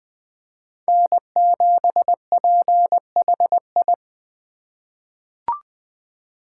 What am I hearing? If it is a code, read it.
Morse code, heard as a keyed tone, about 20 wpm, 700 Hz. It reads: N7PHI
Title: K1USN SST QSO Practice - 20wpm